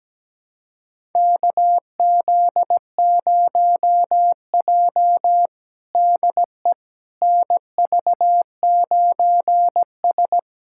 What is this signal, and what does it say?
Morse code, heard as a keyed tone, about 17 wpm, 700 Hz. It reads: KZ0J DE NV9S
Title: USA Callsign DE Callsign - 4 Characters - 17wpm